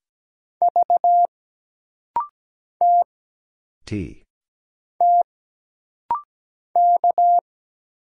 Morse code, heard as a keyed tone, about 17 wpm, 700 Hz. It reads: V T T K
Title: Single Letter-Number Alt - Rapid-Fire 17wpm